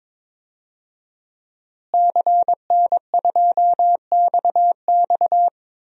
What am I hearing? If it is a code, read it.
Morse code, heard as a keyed tone, about 22 wpm, 700 Hz. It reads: CN2XX
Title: DXCC-100 Call Signs - 5 Characters - Repeated 3x - 22wpm